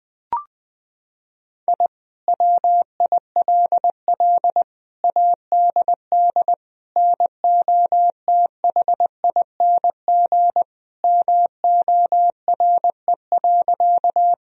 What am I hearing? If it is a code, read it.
Morse code, heard as a keyed tone, about 21 wpm, 700 Hz. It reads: I WILL ADD NOTHING MORE.
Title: Sentences from Top 1000 Words - Review Part 1 - 20wpm